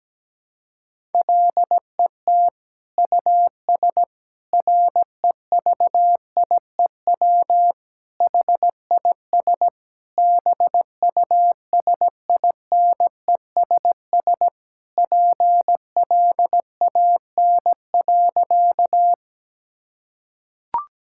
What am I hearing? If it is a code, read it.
Morse code, heard as a keyed tone, about 17 wpm, 700 Hz. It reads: LET US REVIEW HIS BUSINESS PLAN.
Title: Sentences from Top 1000 Words - Review Part 6 - 17wpm